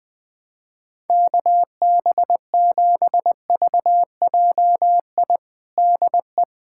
Morse code, heard as a keyed tone, about 20 wpm, 700 Hz. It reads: KB7VJI DE
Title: USA Callsign DE Callsign - 6 Characters - 20wpm